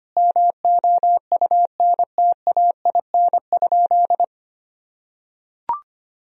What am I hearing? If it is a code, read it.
Morse code, heard as a keyed tone, about 25 wpm, 700 Hz. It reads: MOUNTAIN?
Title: Sentences from Top 1000 Words - Review Part 1 - 25wpm